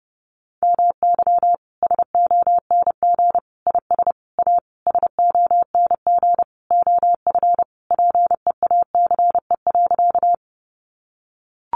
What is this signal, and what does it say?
Morse code, heard as a keyed tone, about 30 wpm, 700 Hz. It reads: MY SONG IS A SONG OF PEACE.
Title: Sentences from Top 1500 Words 30wpm